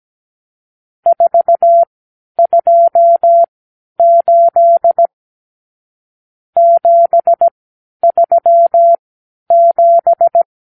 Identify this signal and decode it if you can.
Morse code, heard as a keyed tone, about 17 wpm, spaced as for 10 wpm, 680 Hz. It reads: 428 737
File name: k7qo / Lesson 54 (9)